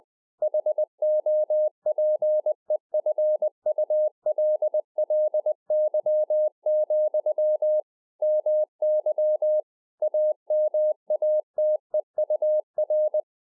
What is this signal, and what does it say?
Morse code, heard as a keyed tone, about 20 wpm, 610 Hz. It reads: HOPEFULLY, MY AMATEUR
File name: 20WPM